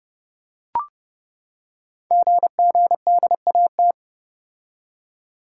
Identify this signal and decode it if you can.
Morse code, heard as a keyed tone, about 30 wpm, 700 Hz. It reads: GGDAT
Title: Code Groups - 5 Letters - 30wpm